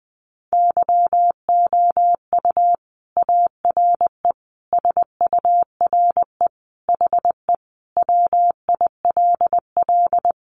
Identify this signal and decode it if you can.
Morse code, heard as a keyed tone, about 19 wpm, 700 Hz. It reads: YOU ARE SURE HE WILL
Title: Sentences from Top 300 Words 20wpm